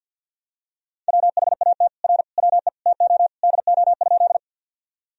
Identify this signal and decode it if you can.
Morse code, heard as a keyed tone, about 53 wpm, 700 Hz. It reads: WHAT R WE TO DO?
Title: Sentences from Top 100 Words and Abbreviations 50wpm